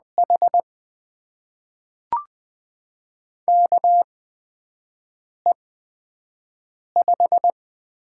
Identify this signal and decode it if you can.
Morse code, heard as a keyed tone, about 20 wpm, spaced as for 5 wpm, 700 Hz. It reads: H KE5
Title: USA Call Signs - 8x Character Spacing - 20wpm